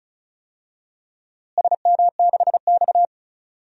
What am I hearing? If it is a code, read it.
Morse code, heard as a keyed tone, about 36 wpm, 700 Hz. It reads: SM6X
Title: DXCC-100 Call Signs - 4 Characters - Repeated 2x - 35wpm